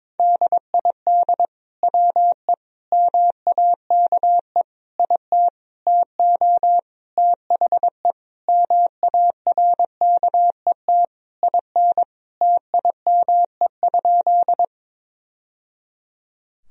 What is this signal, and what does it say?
Morse code, heard as a keyed tone, about 21 wpm, 700 Hz. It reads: DID WE MAKE IT TO THE MARKET IN TIME?
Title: Sentences from Top 1000 Words - Review Part 3 - 22wpm